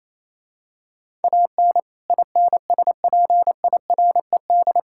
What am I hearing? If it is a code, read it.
Morse code, heard as a keyed tone, about 28 wpm, 700 Hz. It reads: AN INSPIRED
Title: Pangram Challenge v2 - 28wpm